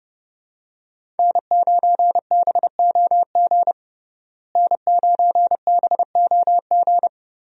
Morse code, heard as a keyed tone, about 30 wpm, 700 Hz. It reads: N9BOG N9BOG
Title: USA Call Signs - 5 Characters - Repeated 3x - 30wpm